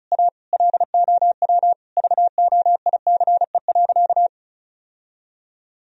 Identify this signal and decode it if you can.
Morse code, heard as a keyed tone, about 36 wpm, 700 Hz. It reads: A LOW VOICE.
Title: Sentences from Top 1000 Words - Review Part 1 - 35wpm